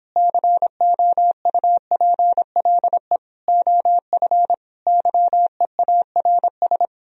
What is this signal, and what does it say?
Morse code, heard as a keyed tone, about 29 wpm, 700 Hz. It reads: COUPLE OF YEARS